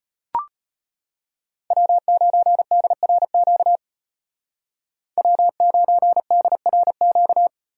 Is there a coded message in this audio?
Yes: Morse code from 1.7 to 7.5 s, heard as a keyed tone, about 34 wpm, 700 Hz. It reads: W9DRQ W9DRQ